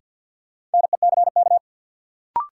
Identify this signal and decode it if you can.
Morse code, heard as a keyed tone, about 49 wpm, 700 Hz. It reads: NECK
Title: Sets of 5 Words Encore - New Word Drill - 50wpm